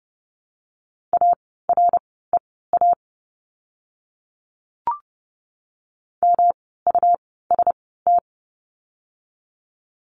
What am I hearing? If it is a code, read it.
Morse code, heard as a keyed tone, about 30 wpm, spaced as for 17 wpm, 700 Hz. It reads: AREA MUST
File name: course / 30:10wpm / 115 Lesson 19f - Letter w - Words 30wpm